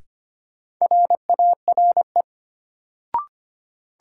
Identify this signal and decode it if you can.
Morse code, heard as a keyed tone, about 25 wpm, 700 Hz. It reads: RARE
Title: Sentences from Top 1100 Words - New Word Drill Alt - 25wpm